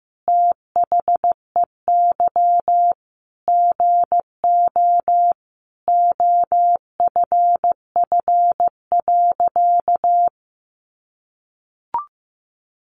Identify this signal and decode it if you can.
Morse code, heard as a keyed tone, about 15 wpm, 700 Hz. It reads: THEY GO OFF.
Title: Sentences from Top 600 Words 15wpm